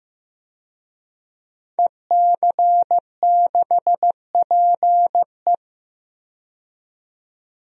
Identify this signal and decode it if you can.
Morse code, heard as a keyed tone, about 15 wpm, 700 Hz. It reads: EC6PE
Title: DXCC-100 Call Signs - 5 Characters - 15wpm